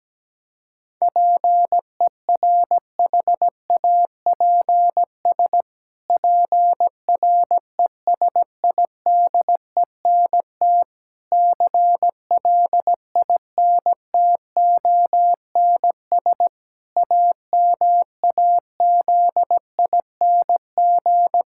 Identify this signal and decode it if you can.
Morse code, heard as a keyed tone, about 17 wpm, 700 Hz. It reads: PERHAPS PRESIDENT CLINTONS AMAZING
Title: Pangram Challenge v2 - 17wpm